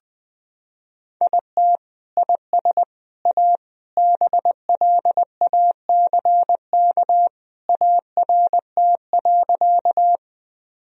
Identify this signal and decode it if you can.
Morse code, heard as a keyed tone, about 21 wpm, 700 Hz. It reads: IT IS A BLACK ART.
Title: Sentences from Top 1000 Words - Review Part 3 - 20wpm